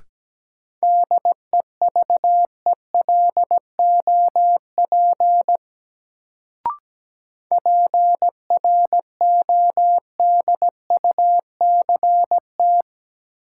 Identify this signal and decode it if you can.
Morse code, heard as a keyed tone, about 17 wpm, 700 Hz. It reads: DEVELOP PRODUCT
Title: Sentences from Top 1000 Words - Review Part 2 - New Word Drill - 17wpm